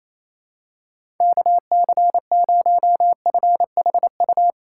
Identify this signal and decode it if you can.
Morse code, heard as a keyed tone, about 28 wpm, 700 Hz. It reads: KC0FHU
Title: USA Call Signs - 6 Characters - 4x Character Spacing - 28wpm